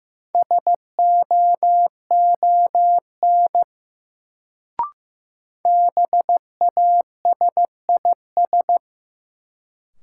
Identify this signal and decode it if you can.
Morse code, heard as a keyed tone, about 15 wpm, 700 Hz. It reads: SOON BASIS
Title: Sets of 3 Words - New Word Drill - 15wpm